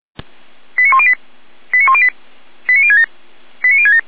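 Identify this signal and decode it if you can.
Five-tone selective call (ZVEI), heard as quick runs of five tones, tones of about 70 ms. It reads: 89108 89108 89067 89067
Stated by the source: ZVEI five-tone selective call received with an SDR.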